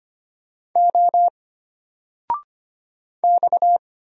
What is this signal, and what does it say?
Morse code, heard as a keyed tone, about 25 wpm, 700 Hz. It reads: O X
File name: Single-Letter-Number-25wpm